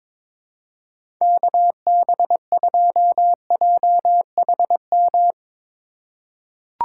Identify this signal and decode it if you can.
Morse code, heard as a keyed tone, about 21 wpm, 700 Hz. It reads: KB2JHM
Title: USA Call Signs - 6 Characters - 2x Character Spacing - 22wpm